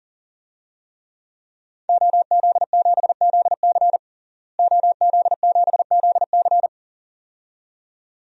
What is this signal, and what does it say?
Morse code, heard as a keyed tone, about 37 wpm, 700 Hz. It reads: OZ7ZC OZ7ZC
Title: DXCC-100 Call Signs - 5 Characters - Repeated 2x - 40wpm